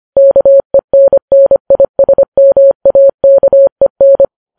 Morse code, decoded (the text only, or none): KENNISMAKEN